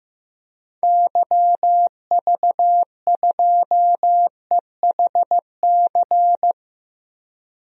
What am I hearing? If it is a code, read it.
Morse code, heard as a keyed tone, about 15 wpm, 700 Hz. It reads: YV2EHC
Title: DXCC-100 Call Signs - 6 Characters - 15wpm